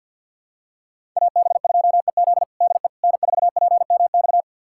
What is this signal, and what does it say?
Morse code, heard as a keyed tone, about 50 wpm, 700 Hz. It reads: AB2EZ DE N4PMX
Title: USA Callsign DE Callsign - 5 Characters - 50wpm